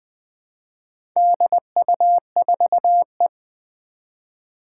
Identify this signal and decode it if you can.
Morse code, heard as a keyed tone, about 20 wpm, 700 Hz. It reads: DU4E
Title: Four Challenging Letter-Number Combinations 20wpm